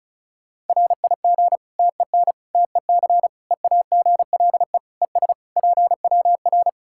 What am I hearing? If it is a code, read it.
Morse code, heard as a keyed tone, about 34 wpm, 700 Hz. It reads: RIG TEN TEC EAGLE ES PWR
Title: K1USN SST QSO Practice 35wpm